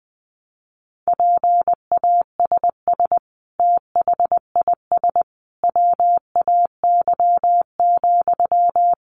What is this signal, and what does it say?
Morse code, heard as a keyed tone, about 20 wpm, 700 Hz. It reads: PASS THIS WAY,
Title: Sentences from Top 400 Words 20wpm